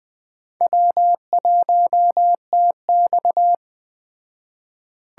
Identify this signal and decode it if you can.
Morse code, heard as a keyed tone, about 20 wpm, 700 Hz. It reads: W1TX
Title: Speed Racing - USA 1x2 Call Signs 17wpm